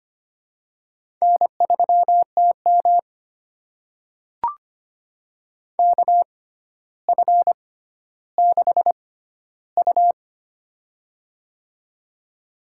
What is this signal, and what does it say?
Morse code, heard as a keyed tone, about 25 wpm, spaced as for 13 wpm, 700 Hz. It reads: N3TM KF6U
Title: USA Call Signs - 4 Characters - 6x Character Spacing - 25wpm